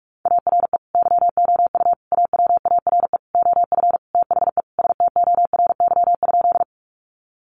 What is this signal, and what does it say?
Morse code, heard as a keyed tone, about 44 wpm, 700 Hz. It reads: ARE YOU AWARE OF THE STORY?
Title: Sentences from Top 1200 Words 45wpm